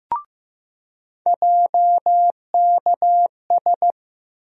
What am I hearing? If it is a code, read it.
Morse code, heard as a keyed tone, about 15 wpm, 700 Hz. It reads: JKS
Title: USA Call Sign Postfixes - 3 Letters 15wpm